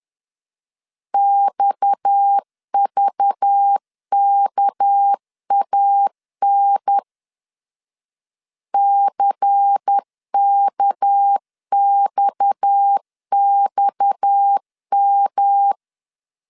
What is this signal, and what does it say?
Morse code, heard as a keyed tone, about 10 wpm, 800 Hz. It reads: XVKAN CKXXM
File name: LEZIONI / 6 PKX0 / 6 PKX 10wpm 2426